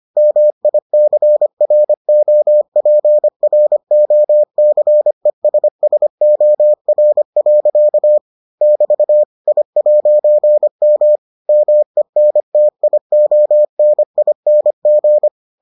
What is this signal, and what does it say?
Morse code, heard as a keyed tone, about 25 wpm, 600 Hz. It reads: MICROPROCESSOR. <BT> I'M MENTIONING